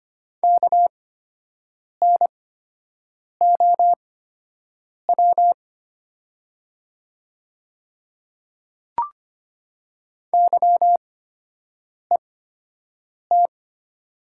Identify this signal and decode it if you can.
Morse code, heard as a keyed tone, about 25 wpm, spaced as for 7 wpm, 700 Hz. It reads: KNOW YET
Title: 8x Character Spacing - Top 500 Words - 25wpm